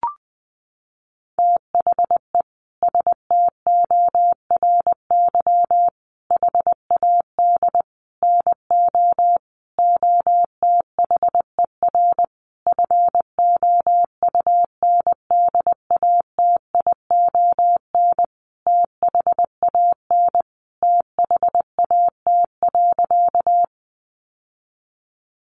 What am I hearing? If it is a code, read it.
Morse code, heard as a keyed tone, about 19 wpm, 700 Hz. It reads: THE STORY HAD NO OTHER FOUNDATION THAN THAT.
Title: Sentences from Top 1200 Words 20wpm